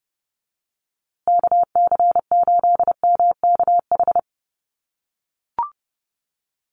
Morse code, heard as a keyed tone, about 30 wpm, 700 Hz. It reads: KC8MKH
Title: USA Call Signs Repeated 2x 30wpm